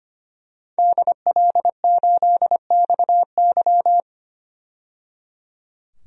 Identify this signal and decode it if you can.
Morse code, heard as a keyed tone, about 25 wpm, 700 Hz. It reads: DL8XY
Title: DXCC-100 Call Signs - 5 Characters - 25wpm